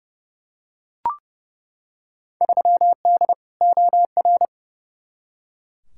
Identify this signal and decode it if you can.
Morse code, heard as a keyed tone, about 32 wpm, 700 Hz. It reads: 3D OR
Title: Field Day Exchanges - 30wpm